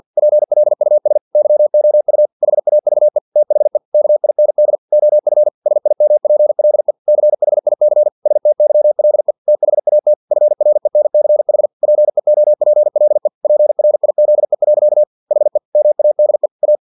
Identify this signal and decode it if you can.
Morse code, heard as a keyed tone, about 50 wpm, 600 Hz. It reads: PLUS YOU HAVE THE KIND OF SIMPLE CHIC STYLE THAT FRENCH PEOPLE PRIZE. HE MADE A
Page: Never 49wpm